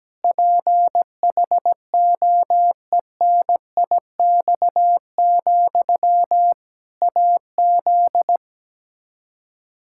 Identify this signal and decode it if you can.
Morse code, heard as a keyed tone, about 17 wpm, 700 Hz. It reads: PHOENIX, AZ
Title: Practice QSO - Exchange 1 - Beginning - 17wpm